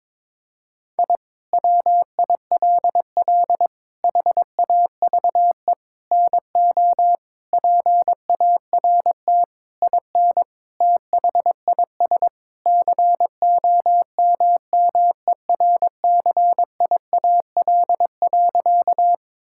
Morse code, heard as a keyed tone, about 22 wpm, 700 Hz. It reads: I WILL HAVE NO PART IN THIS COMMERCIAL.
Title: Sentences from Top 1000 Words - Review Part 5 - 22wpm